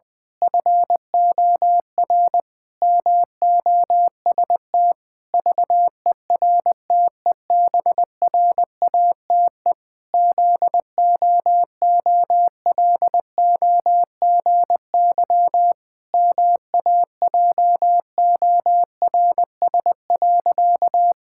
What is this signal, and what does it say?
Morse code, heard as a keyed tone, about 20 wpm, 700 Hz. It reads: FOR MOST VERTEBRATE ZOOLOGY MAJORS.
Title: Pangram Challenge v2 - 20wpm